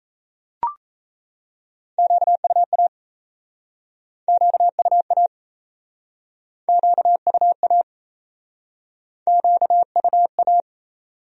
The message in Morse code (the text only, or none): QUA QUA QUA QUA